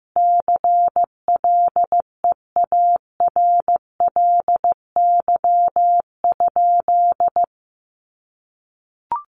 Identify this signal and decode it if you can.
Morse code, heard as a keyed tone, about 15 wpm, 700 Hz. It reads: CLEARLY?
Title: Sentences from Top 1300 Words 15wpm